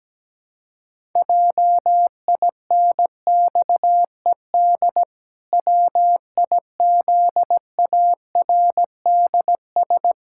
Morse code, heard as a keyed tone, about 17 wpm, 700 Hz. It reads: JINXED WIZARDS
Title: Pangram Challenge v2 - 17wpm